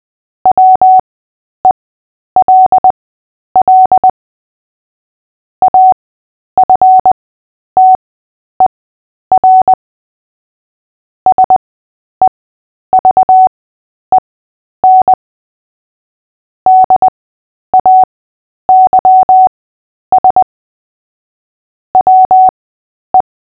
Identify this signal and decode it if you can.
Morse code, heard as a keyed tone, about 20 wpm, spaced as for 10 wpm, 750 Hz. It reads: WELL AFTER SEVEN DAYS WE
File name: cw / fists / Short-Story-8-10wpm